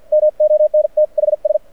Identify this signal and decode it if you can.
Morse code, heard as a keyed tone, about 52 wpm, spaced as for 44 wpm, 590 Hz. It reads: MONTHS
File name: archived / CRSnet